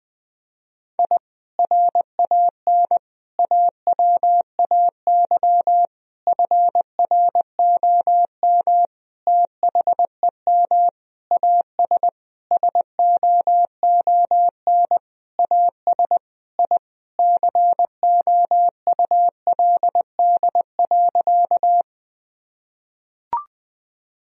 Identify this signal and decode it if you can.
Morse code, heard as a keyed tone, about 20 wpm, 700 Hz. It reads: I RAN AWAY FROM THEM AS SOON AS I COULD.
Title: Sentences from Top 1000 Words - Review Part 3 - 20wpm